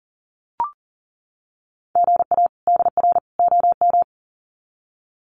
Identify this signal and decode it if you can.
Morse code, heard as a keyed tone, about 40 wpm, 700 Hz. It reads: GA DR OM
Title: Common QSO Phrases 40wpm